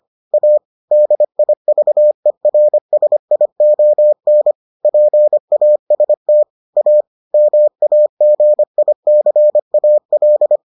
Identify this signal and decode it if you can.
Morse code, heard as a keyed tone, about 25 wpm, 600 Hz. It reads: A DIVERSION PAST A MAGICAL